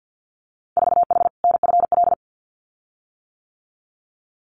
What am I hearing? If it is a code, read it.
Morse code, heard as a keyed tone, about 50 wpm, 700 Hz. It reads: 4H NFL
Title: Winter Field Day Exchanges Encore 50wpm